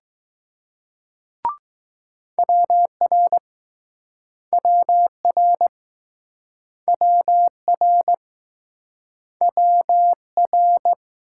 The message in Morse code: WR WR WR WR

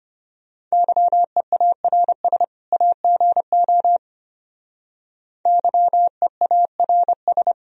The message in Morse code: YEARS AGO YEARS

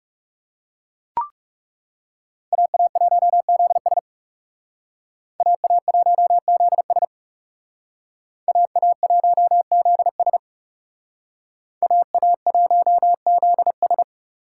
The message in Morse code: AA1ZS AA1ZS AA1ZS AA1ZS

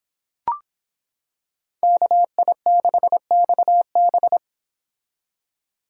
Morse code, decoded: KI6XB

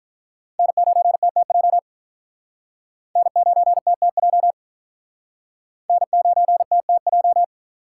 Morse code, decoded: N9TTJ N9TTJ N9TTJ

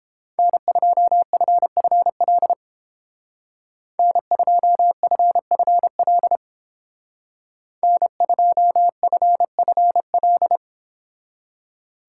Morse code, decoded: N2FFL N2FFL N2FFL

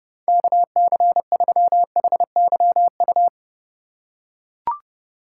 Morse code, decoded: KC3HYU

